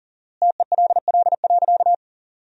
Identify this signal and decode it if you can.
Morse code, heard as a keyed tone, about 40 wpm, 700 Hz. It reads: TELL.
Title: 4x Word-Spacing - Sentences from Top 300 Words - 40wpm